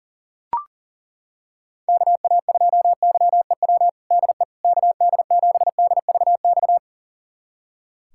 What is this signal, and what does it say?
Morse code, heard as a keyed tone, about 40 wpm, 700 Hz. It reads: KA2YEW DE KD7DVX
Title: USA Callsign DE Callsign - 6 Characters - 40wpm